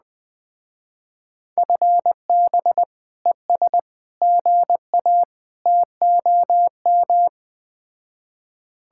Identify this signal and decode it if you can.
Morse code, heard as a keyed tone, about 20 wpm, 700 Hz. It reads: FB ES GA TOM
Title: Speed-Racing Common QSO Phrases - 17wpm